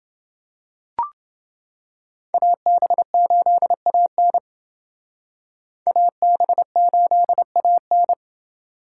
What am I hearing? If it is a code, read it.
Morse code, heard as a keyed tone, about 29 wpm, 700 Hz. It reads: AB8AN AB8AN